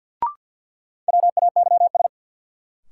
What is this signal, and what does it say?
Morse code, heard as a keyed tone, about 49 wpm, 700 Hz. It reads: WAYS
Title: Sets of 5 Words Encore - New Word Drill - 50wpm